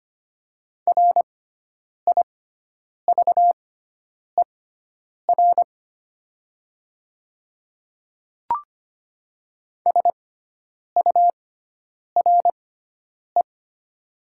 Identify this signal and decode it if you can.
Morse code, heard as a keyed tone, about 25 wpm, spaced as for 9 wpm, 700 Hz. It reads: RIVER SURE
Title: 6x Character Spacing - Top 300 Words - 25wpm